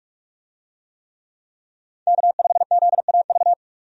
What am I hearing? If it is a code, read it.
Morse code, heard as a keyed tone, about 46 wpm, 700 Hz. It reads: K5ZAV
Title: USA Call Signs - 5 Characters - 45wpm